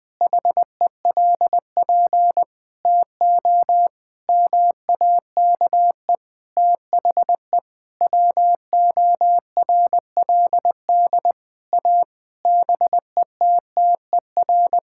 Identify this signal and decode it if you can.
Morse code, heard as a keyed tone, about 20 wpm, 700 Hz. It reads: HELP TO MAKE THE WORLD A BETTER
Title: Sentences with 6-Word Phrases - Spoken Beginning - 20wpm